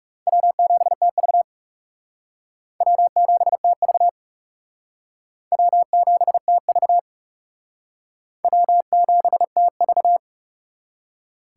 Morse code, heard as a keyed tone, about 38 wpm, 700 Hz. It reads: W7TV W7TV W7TV W7TV